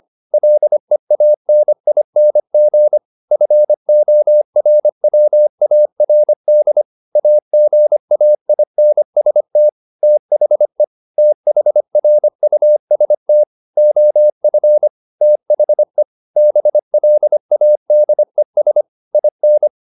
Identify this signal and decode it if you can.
Morse code, heard as a keyed tone, about 25 wpm, 600 Hz. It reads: LEANING FORWARD AGAINST THE THRUST OF THE BLADES IN
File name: the old man and the sea part 1 25wpm